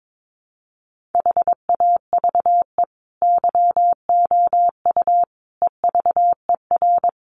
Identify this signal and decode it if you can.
Morse code, heard as a keyed tone, about 22 wpm, 700 Hz. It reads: HAVE YOU EVER
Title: Sentences from Top 1400 Words 22wpm